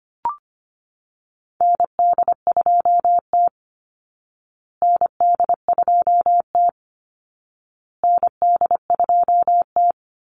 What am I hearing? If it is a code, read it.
Morse code, heard as a keyed tone, about 25 wpm, 700 Hz. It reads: ND2T ND2T ND2T
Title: Call Signs Repeated 2x - 25wpm